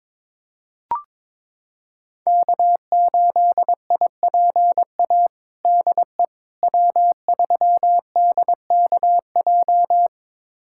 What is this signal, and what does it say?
Morse code, heard as a keyed tone, about 22 wpm, 700 Hz. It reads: K8IPA DE W3DKJ
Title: USA Callsign DE Callsign - 5 Characters - 22wpm